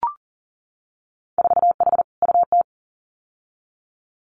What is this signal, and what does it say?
Morse code, heard as a keyed tone, about 38 wpm, 700 Hz. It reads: 4H UT
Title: Winter Field Day Exchanges Encore 40wpm